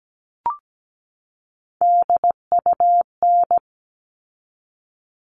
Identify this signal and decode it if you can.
Morse code, heard as a keyed tone, about 17 wpm, 700 Hz. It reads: DUN
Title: All Cut Numbers - 3-Digit 17wpm